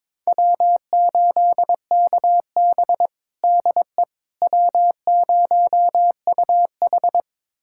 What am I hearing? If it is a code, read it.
Morse code, heard as a keyed tone, about 22 wpm, 700 Hz. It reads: W8KB DE W0UH
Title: USA Callsign DE Callsign - 4 Characters - 22wpm